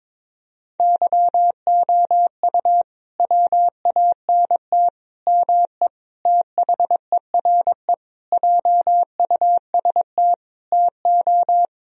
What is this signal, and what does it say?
Morse code, heard as a keyed tone, about 22 wpm, 700 Hz. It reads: YOU WANT ME THERE JUST TO
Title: Sentences from Top 1000 Words - Review Part 2 - 22wpm